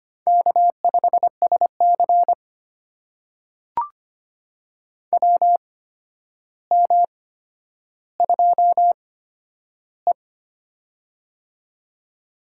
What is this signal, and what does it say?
Morse code, heard as a keyed tone, about 25 wpm, spaced as for 11 wpm, 700 Hz. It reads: K5SC WM2E
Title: USA Call Signs - 4 Characters - 8x Character Spacing - 25wpm